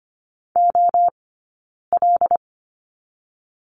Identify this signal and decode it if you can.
Morse code, heard as a keyed tone, about 25 wpm, 700 Hz. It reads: O L
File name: course / 25:10wpm / 092 Lesson 16d - Letter c - Two Characters 25wpm